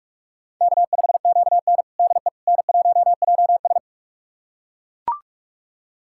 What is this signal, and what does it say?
Morse code, heard as a keyed tone, about 42 wpm, 700 Hz. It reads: K5QN DE N1JS
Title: USA Callsign DE Callsign - 4 Characters - 45wpm